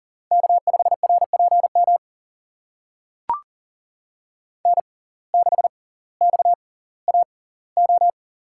Morse code, heard as a keyed tone, about 40 wpm, spaced as for 20 wpm, 700 Hz. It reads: K5RPM N6XAO
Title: USA Call Signs - 5 Characters - 6x Character Spacing - 40wpm